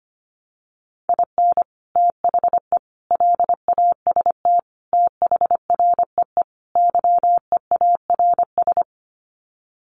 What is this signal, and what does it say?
Morse code, heard as a keyed tone, about 25 wpm, 700 Hz. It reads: IN THE LAST THREE YEARS
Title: Sets of 5 Words Encore2 25wpm